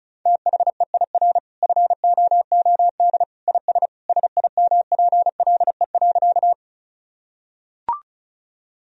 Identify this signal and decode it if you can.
Morse code, heard as a keyed tone, about 34 wpm, 700 Hz. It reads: THEIR FOOD IS SIMPLE.
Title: Sentences from Top 1000 Words - Review Part 1 - 35wpm